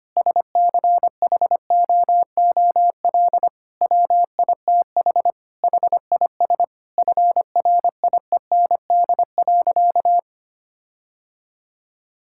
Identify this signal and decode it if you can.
Morse code, heard as a keyed tone, about 25 wpm, 700 Hz. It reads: SCHOOL WITH HIS FRIEND.
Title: Sentences from Top 1600 Words 25wpm